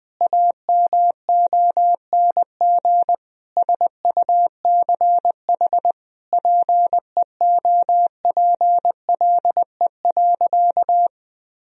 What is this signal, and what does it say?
Morse code, heard as a keyed tone, about 20 wpm, 700 Hz. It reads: AMONG SUCH PEOPLE.
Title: Sentences from Top 1000 Words - Review Part 3 - 20wpm